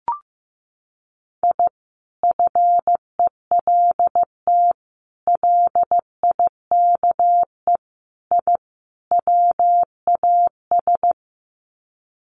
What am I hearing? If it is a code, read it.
Morse code, heard as a keyed tone, about 15 wpm, 700 Hz. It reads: I FELT LIKE I WAS
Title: Sets of 5 Words Encore 15wpm